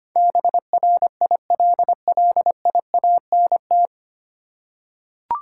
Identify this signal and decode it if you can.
Morse code, heard as a keyed tone, about 25 wpm, 700 Hz. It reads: BRILLIANT
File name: Nine Letter Words - 25wpm